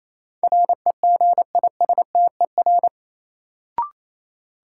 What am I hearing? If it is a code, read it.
Morse code, heard as a keyed tone, about 28 wpm, 700 Hz. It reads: REGISTER
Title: Sentences from Top 1200 Words - New Word Drill - 28wpm